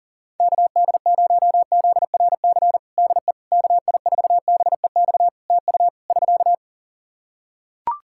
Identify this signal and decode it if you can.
Morse code, heard as a keyed tone, about 40 wpm, 700 Hz. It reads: KD0ZRC DE KI4BEX TU <SK>
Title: Practice QSO - Exchange 4 - Ending - 40wpm